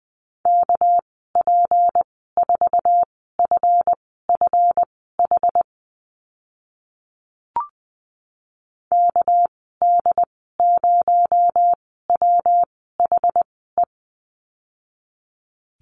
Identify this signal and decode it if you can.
Morse code, heard as a keyed tone, about 20 wpm, spaced as for 14 wpm, 700 Hz. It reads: KP4FFH KD0WHE
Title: 219 Lesson 35g - Number 0 - Call Signs 20wpm